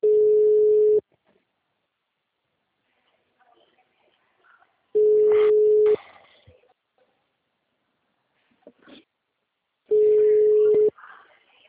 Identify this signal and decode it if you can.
Ringback tone (Europe, 425 Hz), with speech.